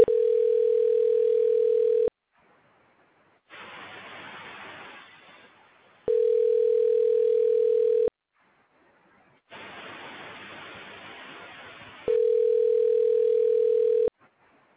Ringback tone (North America), 440 + 480 Hz.